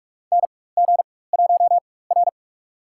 Morse code, heard as a keyed tone, about 43 wpm, spaced as for 21 wpm, 700 Hz. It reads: NG1R
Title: USA Call Signs - 4x Character Spacing - 45wpm